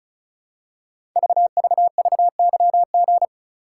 Morse code, heard as a keyed tone, about 36 wpm, 700 Hz. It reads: VVVYG
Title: Code Groups - 5 Letters - 35wpm